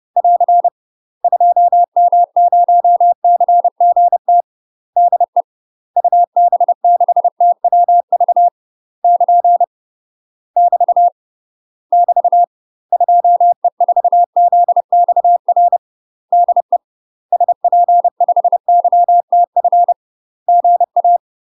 Morse code, heard as a keyed tone, about 30 wpm, 700 Hz. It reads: <AR> 2M0CGT DE UB6TWV <KN> <BT> <BT> 2E4ZXR DE SP5YTF GA
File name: UK MASTER QSOs TRAINING/30WPM